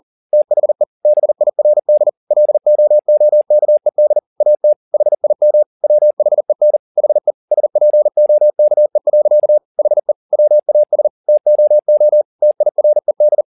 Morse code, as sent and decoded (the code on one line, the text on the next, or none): - .... / -... .. .-. -.. / .-.. --- --- -.- . -.. / .- - / .... .. -- / .-- .... -. / .... / ... .--. --- -.- . .-.-.- / .... / .-- .- ... / - --- --- / - .. .-. -..
THE BIRD LOOKED AT HIM WHEN HE SPOKE. HE WAS TOO TIRED